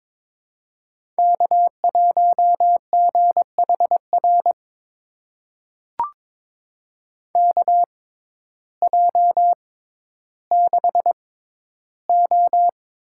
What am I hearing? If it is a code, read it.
Morse code, heard as a keyed tone, about 22 wpm, spaced as for 12 wpm, 700 Hz. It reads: K1GHR KJ6O
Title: USA Call Signs - 5 Characters - 6x Character Spacing - 22wpm